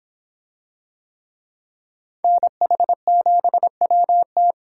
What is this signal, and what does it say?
Morse code, heard as a keyed tone, about 24 wpm, 700 Hz. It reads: NH7WT